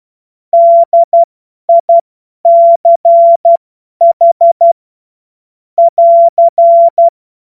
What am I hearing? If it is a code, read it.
Morse code, heard as a keyed tone, about 12 wpm, spaced as for 10 wpm, 680 Hz. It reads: DICH <AR>
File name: Morsekurs / Witze / Witz 07. Juni 2022 50 F70